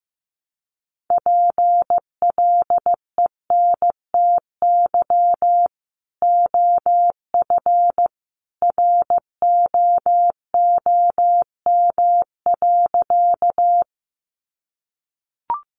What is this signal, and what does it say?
Morse code, heard as a keyed tone, about 15 wpm, 700 Hz. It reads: PLENTY OF ROOM.
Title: Sentences from Top 1300 Words 15wpm